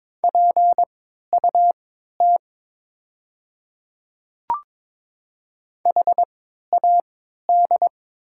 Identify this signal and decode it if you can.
Morse code, heard as a keyed tone, about 22 wpm, spaced as for 12 wpm, 700 Hz. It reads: PUT HAD
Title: 3x Character Spacing - Top 500 Words - 22wpm